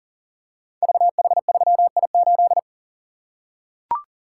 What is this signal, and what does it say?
Morse code, heard as a keyed tone, about 42 wpm, 700 Hz. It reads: VH3I8
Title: Code Groups - 5 Letter:Numbers - 40wpm